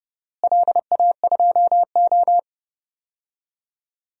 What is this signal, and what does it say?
Morse code, heard as a keyed tone, about 30 wpm, 700 Hz. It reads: LA2O
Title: DXCC-100 Call Signs - 4 Characters - Repeated 2x - 30wpm